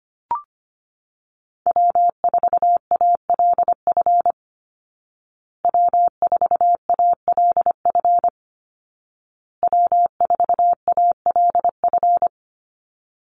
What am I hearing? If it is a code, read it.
Morse code, heard as a keyed tone, about 25 wpm, 700 Hz. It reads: W4ALF W4ALF W4ALF
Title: Call Signs Repeated 2x - 25wpm